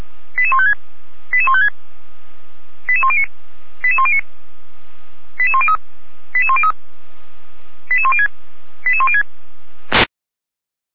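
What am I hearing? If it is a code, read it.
Five-tone selective call (ZVEI), heard as quick runs of five tones, tones of about 70 ms. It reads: 88156 88156 88109 88109 88103 88103 88106 88106
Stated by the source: ZVEI five-tone selective call received by SDR, from the Feuerwehr Orlamünde fire brigade server.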